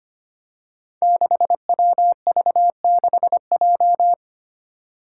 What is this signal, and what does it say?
Morse code, heard as a keyed tone, about 25 wpm, 700 Hz. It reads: 6WV6J
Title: Five Challenging Letter-Number Combinations 25wpm